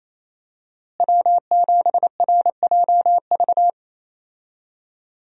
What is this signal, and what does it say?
Morse code, heard as a keyed tone, about 28 wpm, 700 Hz. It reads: W7RJV
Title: USA Call Signs - 5 Characters - Repeated 3x - 28wpm